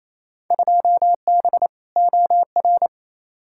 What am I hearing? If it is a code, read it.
Morse code, heard as a keyed tone, about 28 wpm, 700 Hz. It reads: 2B OR
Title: Field Day Exchanges Encore - 28wpm